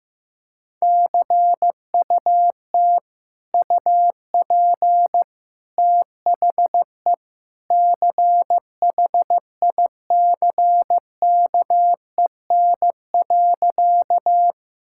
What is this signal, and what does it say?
Morse code, heard as a keyed tone, about 16 wpm, 700 Hz. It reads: CUT UP THE CHICKEN.
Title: Sentences from Top 1000 Words - Review Part 5 - 15wpm